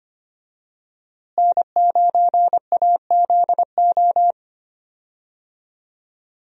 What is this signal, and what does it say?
Morse code, heard as a keyed tone, about 25 wpm, 700 Hz. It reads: N9AZO
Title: USA Call Signs - 5 Characters - Repeated 2x - 25wpm